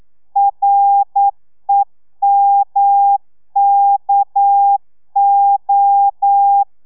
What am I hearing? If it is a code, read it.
Morse code, heard as a keyed tone, about 9 wpm, 800 Hz. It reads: REMKO
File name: telegrafia / i2tfj / Lez6 40